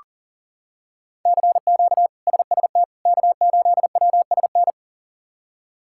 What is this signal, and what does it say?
Morse code, heard as a keyed tone, about 40 wpm, 700 Hz. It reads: CQ SST K8WSN
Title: K1USN SST QSO Practice - 40wpm